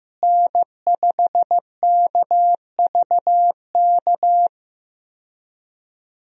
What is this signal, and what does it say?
Morse code, heard as a keyed tone, about 15 wpm, 700 Hz. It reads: N5KVK
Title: USA Call Signs - 5 Characters - Repeated 3x - 15wpm